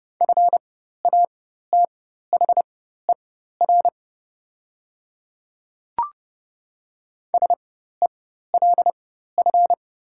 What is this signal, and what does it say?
Morse code, heard as a keyed tone, about 30 wpm, spaced as for 14 wpm, 700 Hz. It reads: FATHER SELF
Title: 4x Character Spacing - Top 300 Words - 30wpm